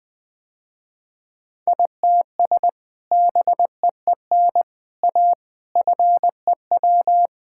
Morse code, heard as a keyed tone, about 20 wpm, 700 Hz. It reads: ITS BEEN A FEW
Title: Sentences with 3-Word Phrases - Spoken Beginning - 20wpm